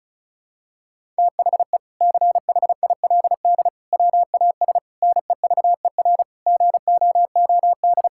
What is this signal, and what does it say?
Morse code, heard as a keyed tone, about 36 wpm, 700 Hz. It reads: THE CHILD WAS NEVER GOOD
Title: Sentences from Top 1000 Words - Review Part 6 - 35wpm